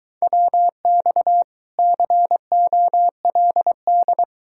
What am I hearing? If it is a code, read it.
Morse code, heard as a keyed tone, about 23 wpm, 700 Hz. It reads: WX COLD